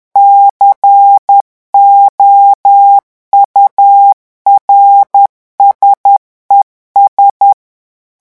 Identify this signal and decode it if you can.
Morse code, heard as a keyed tone, about 10 wpm, 800 Hz. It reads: COURSES
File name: revision9 citations humoristiques vitesse 10 mots minute